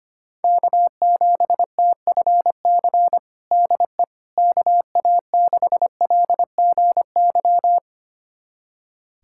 Morse code, heard as a keyed tone, about 25 wpm, 700 Hz. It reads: K7TFC DE KA6LGY
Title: USA Callsign DE Callsign 25wpm